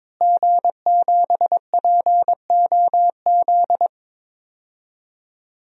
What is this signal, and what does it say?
Morse code, heard as a keyed tone, about 22 wpm, 700 Hz. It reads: G7POZ
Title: DXCC-100 Call Signs - 5 Characters - Repeated 3x - 22wpm